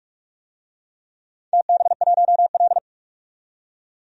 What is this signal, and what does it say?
Morse code, heard as a keyed tone, about 46 wpm, 700 Hz. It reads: TB1L
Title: DXCC-100 Call Signs - 4 Characters - 45wpm